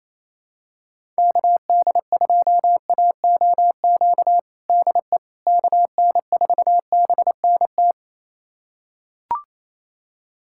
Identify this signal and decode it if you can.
Morse code, heard as a keyed tone, about 28 wpm, 700 Hz. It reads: KD2AOQ DE KN4BNT
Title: USA Callsign DE Callsign - 6 Characters - 28wpm